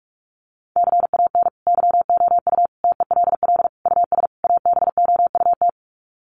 Morse code, heard as a keyed tone, about 44 wpm, 700 Hz. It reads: CAN YOU TELL US ABOUT
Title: Sets of 5 Words Encore2 45wpm